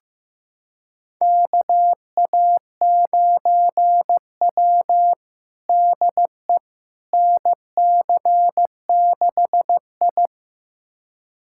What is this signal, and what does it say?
Morse code, heard as a keyed tone, about 15 wpm, 700 Hz. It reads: KA9W DE NC6I
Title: USA Callsign DE Callsign - 4 Characters - 15wpm